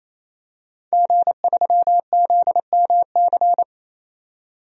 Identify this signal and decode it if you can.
Morse code, heard as a keyed tone, about 28 wpm, 700 Hz. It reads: G3ZMC
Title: DXCC-100 Call Signs - 5 Characters - 28wpm